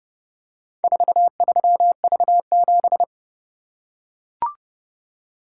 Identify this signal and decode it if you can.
Morse code, heard as a keyed tone, about 28 wpm, 700 Hz. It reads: 43V7